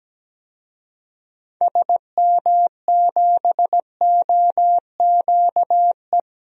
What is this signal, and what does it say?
Morse code, heard as a keyed tone, about 17 wpm, 700 Hz. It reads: SM7OQE
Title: DXCC-100 Call Signs - 6 Characters - 17wpm